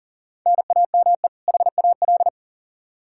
Morse code, heard as a keyed tone, about 40 wpm, 700 Hz. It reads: NAME HAL
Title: Practice QSO - Exchange 1 - Beginning - 40wpm